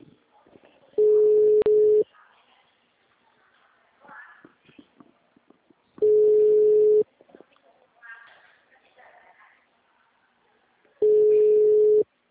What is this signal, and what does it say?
Ringback tone (Europe, 425 Hz), with speech.
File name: out- -2018- - - .70091